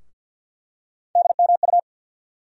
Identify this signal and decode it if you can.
Morse code, heard as a keyed tone, about 49 wpm, 690 Hz. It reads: DMU
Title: USA Call Sign Postfixes - 3 Letters 50wpm